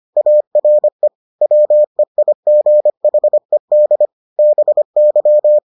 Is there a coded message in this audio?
Yes: Morse code from 0.2 to 5.6 s, heard as a keyed tone, about 25 wpm, 600 Hz. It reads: ARE WEIGHED BY